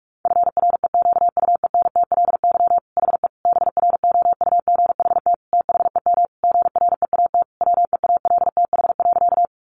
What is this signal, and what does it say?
Morse code, heard as a keyed tone, about 44 wpm, 700 Hz. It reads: FREQUENTLY HE BROUGHT THEM GREAT WEALTH.
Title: Sentences from Top 1300 Words 45wpm